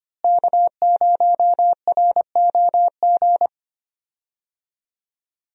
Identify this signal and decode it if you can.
Morse code, heard as a keyed tone, about 25 wpm, 700 Hz. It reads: K0ROG
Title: USA Call Signs - 5 Characters - Repeated 3x - 25wpm